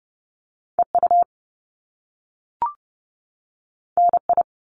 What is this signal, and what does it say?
Morse code, heard as a keyed tone, about 30 wpm, 700 Hz. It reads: EU NI
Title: Top Two Letter Combinations in Words 30wpm